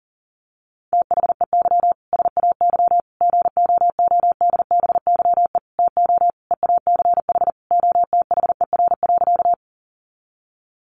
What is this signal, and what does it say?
Morse code, heard as a keyed tone, about 40 wpm, 700 Hz. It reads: THEY SAY GOODBYE TO EACH OTHER.
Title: Additional Easy Phrases 40wpm